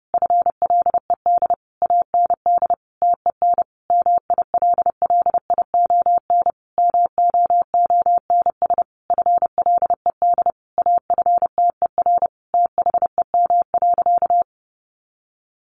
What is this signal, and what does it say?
Morse code, heard as a keyed tone, about 30 wpm, 700 Hz. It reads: FLED AND TEN MILLION MOONS FLED AFTER THEM.
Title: A Sound Of Thunder 30wpm